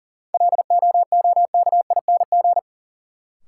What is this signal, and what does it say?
Morse code, heard as a keyed tone, about 40 wpm, 700 Hz. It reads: LOOKING
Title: Sentences from Top 1000 Words - Review Part 6 - New Word Drill - 40wpm